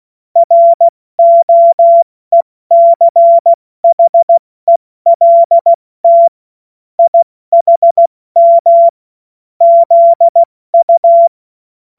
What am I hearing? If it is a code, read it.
Morse code, heard as a keyed tone, about 16 wpm, 680 Hz. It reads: ROECHELT IHM ZU
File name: Morsekurs / Witze / Witz 22. Juni 2021, 70 F80